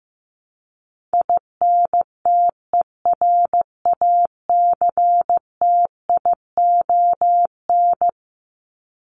Morse code, heard as a keyed tone, about 15 wpm, 700 Hz. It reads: INTERACTION
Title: Top 1251-1500 Words 15wpm